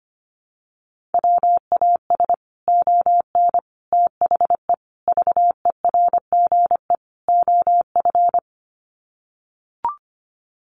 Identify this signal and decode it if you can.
Morse code, heard as a keyed tone, about 25 wpm, 700 Hz. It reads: WAS ON THE VERGE OF
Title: Sets of 5 Words Encore 25wpm